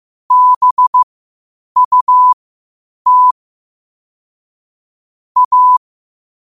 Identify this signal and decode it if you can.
Morse code, heard as a keyed tone, about 15 wpm, spaced as for 8 wpm, 1000 Hz. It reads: BUT A
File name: qotdfiles / 8wpm